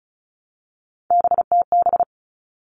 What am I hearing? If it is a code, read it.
Morse code, heard as a keyed tone, about 34 wpm, 700 Hz. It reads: BTB
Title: All Cut Numbers - 3-Digit 35wpm